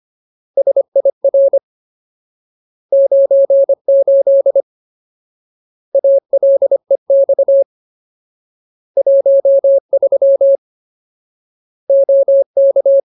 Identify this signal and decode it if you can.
Morse code, heard as a keyed tone, about 25 wpm, 550 Hz. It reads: SIR 98 ALEX 13 OK
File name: Ultimate Word List / 25WPM Ultimate